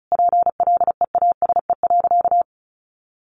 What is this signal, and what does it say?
Morse code, heard as a keyed tone, about 34 wpm, 700 Hz. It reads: PLEASE.
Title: Sentences from Top 1500 Words 35wpm